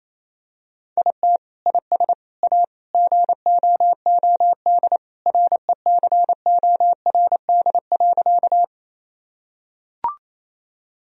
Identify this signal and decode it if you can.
Morse code, heard as a keyed tone, about 27 wpm, 700 Hz. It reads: IT IS A GOOD RECORD.
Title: Sentences from Top 1000 Words - Review Part 3 - 28wpm